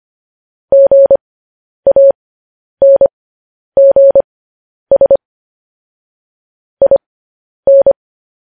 Morse code, heard as a keyed tone, about 25 wpm, spaced as for 10 wpm, 550 Hz. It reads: GANGS IN